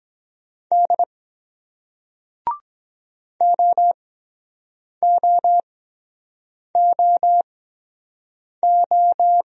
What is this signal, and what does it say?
Morse code, heard as a keyed tone, about 23 wpm, spaced as for 7 wpm, 700 Hz. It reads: D OOOO